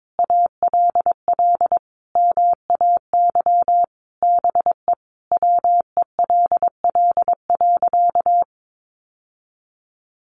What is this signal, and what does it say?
Morse code, heard as a keyed tone, about 22 wpm, 700 Hz. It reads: ALL MAY BE WELL.
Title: Sentences from Top 1200 Words 22wpm